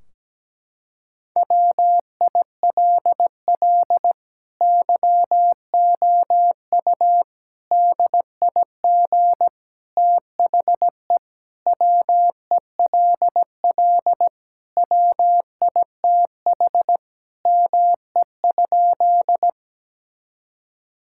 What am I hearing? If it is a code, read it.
Morse code, heard as a keyed tone, about 17 wpm, 700 Hz. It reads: WILL YOU DIG THE WELL WITH ME?
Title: Sentences from Top 1600 Words 17wpm